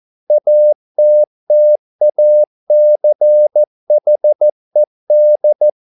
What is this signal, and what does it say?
Morse code, heard as a keyed tone, about 14 wpm, 600 Hz. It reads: ATTACHED